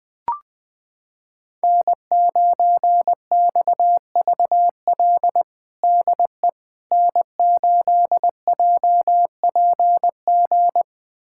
Morse code, heard as a keyed tone, about 21 wpm, 700 Hz. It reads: N9XVL DE N8JPG
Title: USA Callsign DE Callsign - 5 Characters - 20wpm